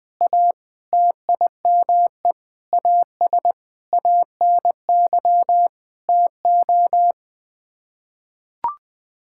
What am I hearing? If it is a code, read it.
Morse code, heard as a keyed tone, about 20 wpm, 700 Hz. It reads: A TIME AS ANY TO
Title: Sets of 8 Words - 20wpm